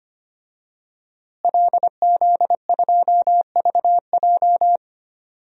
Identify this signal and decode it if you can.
Morse code, heard as a keyed tone, about 25 wpm, 700 Hz. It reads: LZ2VJ
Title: DXCC-100 Call Signs - 5 Characters - Repeated 2x - 25wpm